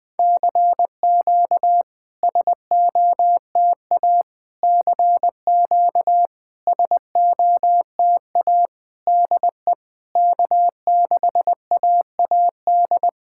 Morse code, heard as a keyed tone, about 20 wpm, 700 Hz. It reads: CQ SOTA CQ SOTA DE K6AAD
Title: SOTA QSO Practice 20wpm